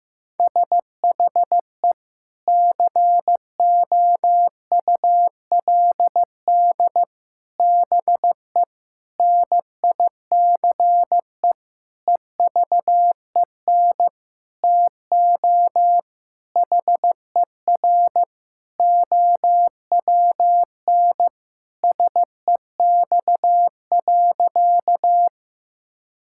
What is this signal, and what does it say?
Morse code, heard as a keyed tone, about 16 wpm, 700 Hz. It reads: SHE COULD BE NICE EVEN TO HER OWN SEX.
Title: Sentences from Top 1600 Words 15wpm